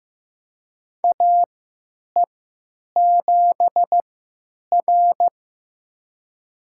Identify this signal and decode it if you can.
Morse code, heard as a keyed tone, about 15 wpm, spaced as for 9 wpm, 700 Hz. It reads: AE7R
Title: USA Call Signs - 4 Characters - 3x Character Spacing - 15wpm